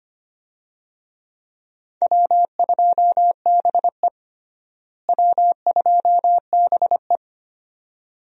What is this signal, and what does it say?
Morse code, heard as a keyed tone, about 25 wpm, 700 Hz. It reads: W2BE W2BE
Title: USA Call Signs - 4 Characters - Repeated 2x - 25wpm